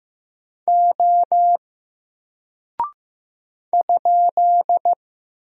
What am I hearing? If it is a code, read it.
Morse code, heard as a keyed tone, about 15 wpm, 700 Hz. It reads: O ?
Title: Single Character (Letter:Number:Punctuation:Prosign) Alt 15wpm